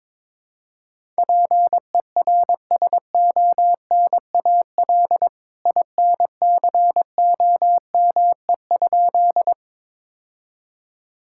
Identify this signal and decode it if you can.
Morse code, heard as a keyed tone, about 22 wpm, 700 Hz. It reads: PERSONAL INCOME?
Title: Sentences from Top 1000 Words - Review Part 5 - 22wpm